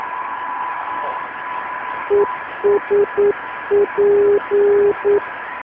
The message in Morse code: ESP

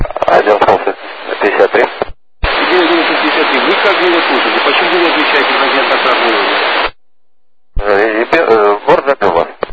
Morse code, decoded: none

